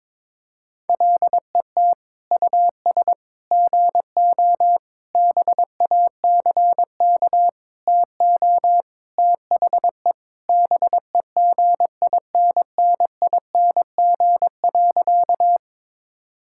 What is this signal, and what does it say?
Morse code, heard as a keyed tone, about 22 wpm, 700 Hz. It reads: LET US GO BACK TO THE BEGINNING.
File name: practice / 22wpm / Sentences with 4-Word Phrases - 22wpm